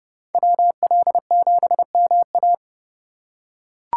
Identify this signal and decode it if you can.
Morse code, heard as a keyed tone, about 31 wpm, 700 Hz. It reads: WL7MA
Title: USA Call Signs - 3x Character Spacing - 30wpm